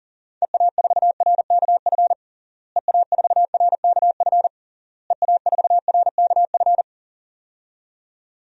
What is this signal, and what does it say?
Morse code, heard as a keyed tone, about 40 wpm, 700 Hz. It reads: EA4RKF EA4RKF EA4RKF
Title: DXCC-100 Call Signs - 6 Characters - Repeated 3x - 40wpm